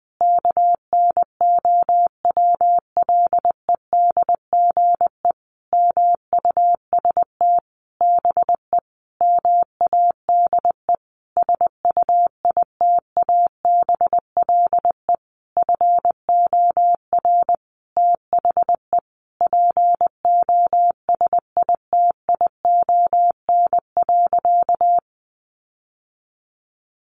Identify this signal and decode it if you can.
Morse code, heard as a keyed tone, about 20 wpm, 700 Hz. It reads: KNOWLEDGE MUST BE MADE SUITABLE FOR THE POSITION.
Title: Sentences from Top 1400 Words 20wpm